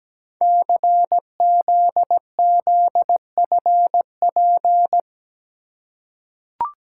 Morse code, heard as a keyed tone, about 17 wpm, 700 Hz. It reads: CZZFP
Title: Code Groups - 5 Letter:Numbers - 17wpm